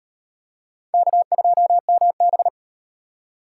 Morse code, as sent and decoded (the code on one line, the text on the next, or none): -.- ..--- -- -...
K2MB